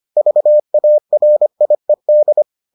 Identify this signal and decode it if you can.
Morse code, heard as a keyed tone, about 25 wpm, 600 Hz. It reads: VARIED